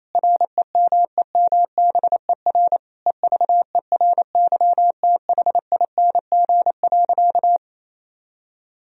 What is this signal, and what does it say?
Morse code, heard as a keyed tone, about 28 wpm, 700 Hz. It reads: REMEMBER EVERYTHING.
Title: Sentences from Top 1000 Words - Review Part 1 - 28wpm